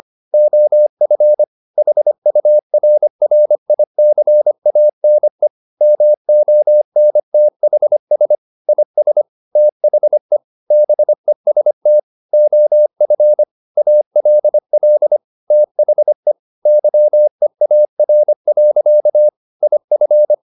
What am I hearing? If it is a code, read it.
Morse code, heard as a keyed tone, about 25 wpm, 600 Hz. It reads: OF HURRICANE MONTHS IS THE BEST OF ALL THE YEAR. IF
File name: the old man and the sea part 1 25wpm